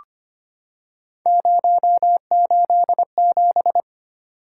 Morse code, heard as a keyed tone, about 25 wpm, 700 Hz. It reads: 087
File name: Three Letter-Number Combinations - 25wpm